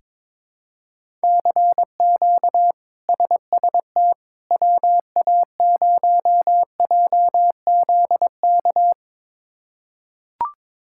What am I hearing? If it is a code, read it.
Morse code, heard as a keyed tone, about 22 wpm, 700 Hz. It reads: CQ SST WA0JZK
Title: K1USN SST QSO Practice - 22wpm